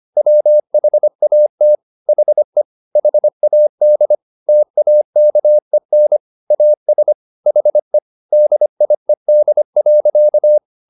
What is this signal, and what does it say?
Morse code, heard as a keyed tone, about 25 wpm, 600 Hz. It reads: WHAT HE HAD TAKEN AS HE DIED.